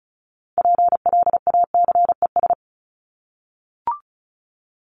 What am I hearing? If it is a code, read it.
Morse code, heard as a keyed tone, about 34 wpm, 700 Hz. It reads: PLACES
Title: ES:S Suffix - Top 100 Words with Base Words 35wpm